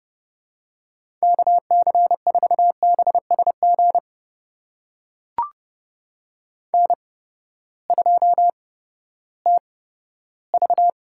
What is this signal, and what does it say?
Morse code, heard as a keyed tone, about 30 wpm, spaced as for 15 wpm, 700 Hz. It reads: KC4BSG N2TV
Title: USA Call Signs - 8x Character Spacing - 30wpm